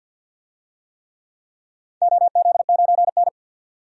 Morse code, heard as a keyed tone, about 52 wpm, 700 Hz. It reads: OZ9N
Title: DXCC-100 Call Signs - 4 Characters - 50wpm